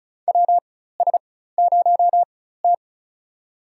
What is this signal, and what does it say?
Morse code, heard as a keyed tone, about 36 wpm, spaced as for 16 wpm, 700 Hz. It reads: WS0T
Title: USA Call Signs - 4 Characters - 4x Character Spacing - 35wpm